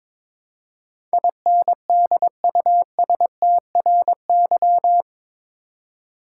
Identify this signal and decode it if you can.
Morse code, heard as a keyed tone, about 22 wpm, 700 Hz. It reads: INDUSTRY
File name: practice / 22wpm / Eight Letter Words - 22wpm